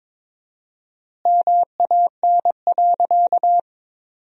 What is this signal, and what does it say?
Morse code, heard as a keyed tone, about 22 wpm, 700 Hz. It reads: MAN.
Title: 4x Word-Spacing - Sentences from Top 500 Words - 22wpm